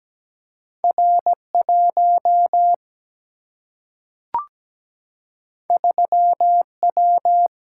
Morse code, heard as a keyed tone, about 17 wpm, 700 Hz. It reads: R1 3W
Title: Single-Letter Single-Number 17wpm